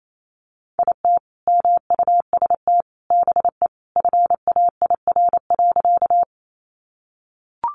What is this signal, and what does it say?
Morse code, heard as a keyed tone, about 28 wpm, 700 Hz. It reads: IT MUST BE FAIR.
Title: Sentences from Top 1300 Words 28wpm